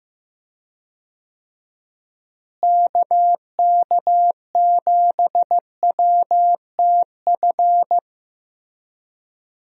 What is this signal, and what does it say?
Morse code, heard as a keyed tone, about 15 wpm, 700 Hz. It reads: KK7WTF
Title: USA Call Signs - 6 Characters - Repeated 2x - 15wpm